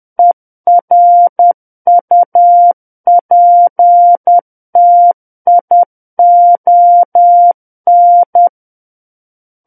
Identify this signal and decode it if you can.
Morse code, heard as a keyed tone, about 10 wpm, 700 Hz. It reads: ERUPTION
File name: Book One / 01-Chapter 1 - 10 WPM